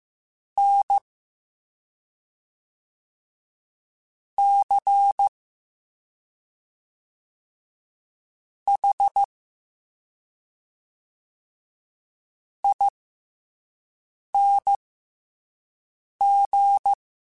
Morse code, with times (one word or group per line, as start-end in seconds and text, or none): 0.6-1.0 N
4.4-5.3 C
8.7-9.2 H
12.6-16.9 ING